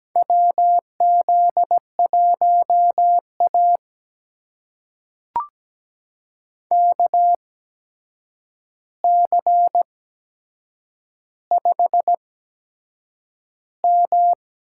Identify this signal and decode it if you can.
Morse code, heard as a keyed tone, about 17 wpm, spaced as for 7 wpm, 700 Hz. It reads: WZ1A KC5M
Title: USA Call Signs - 4 Characters - 8x Character Spacing - 17wpm